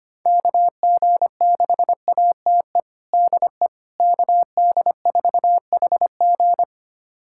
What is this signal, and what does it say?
Morse code, heard as a keyed tone, about 25 wpm, 700 Hz. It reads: KG6ATE DE KD4HG
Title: USA Callsign DE Callsign 25wpm